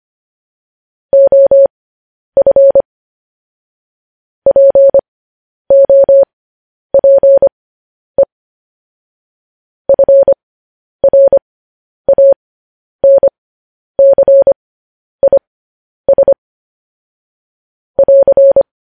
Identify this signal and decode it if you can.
Morse code, heard as a keyed tone, about 25 wpm, spaced as for 10 wpm, 550 Hz. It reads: OF POPE FRANCIS <AR>